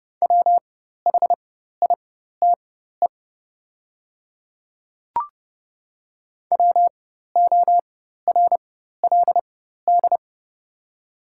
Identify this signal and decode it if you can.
Morse code, heard as a keyed tone, about 31 wpm, spaced as for 14 wpm, 690 Hz. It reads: WHITE WORLD
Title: 4x Character Spacing - Top 300 Words - 30wpm